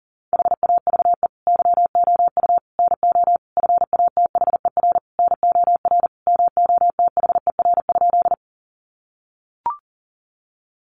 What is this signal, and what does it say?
Morse code, heard as a keyed tone, about 40 wpm, 700 Hz. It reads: HAVE YOU NO FATHER NOR MOTHER?
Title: Sentences from Top 1500 Words 40wpm